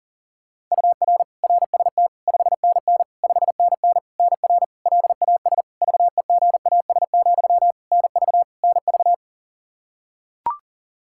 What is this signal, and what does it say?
Morse code, heard as a keyed tone, about 40 wpm, 700 Hz. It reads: UR RST 5NN 5NN NR LAS VEGAS, NV NV
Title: K1USN SST QSO Practice 40wpm